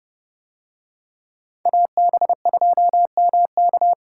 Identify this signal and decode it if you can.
Morse code, heard as a keyed tone, about 29 wpm, 700 Hz. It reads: AB2MK